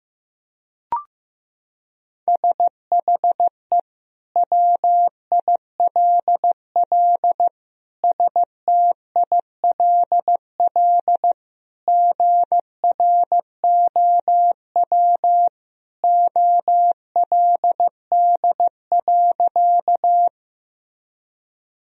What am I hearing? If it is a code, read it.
Morse code, heard as a keyed tone, about 15 wpm, 700 Hz. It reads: SHE WILL STILL GROW OLD.
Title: Sentences from Top 1000 Words - Review Part 1 - 15wpm